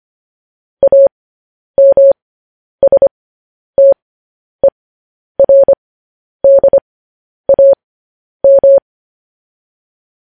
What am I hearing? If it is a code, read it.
Morse code, heard as a keyed tone, about 25 wpm, spaced as for 10 wpm, 550 Hz. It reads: AMSTERDAM
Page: news-headlines-in-morse-code-at-10-wpm